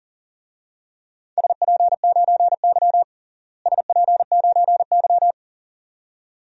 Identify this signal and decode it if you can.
Morse code, heard as a keyed tone, about 37 wpm, 700 Hz. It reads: SP9Y SP9Y
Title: DXCC-100 Call Signs - 4 Characters - Repeated 2x - 40wpm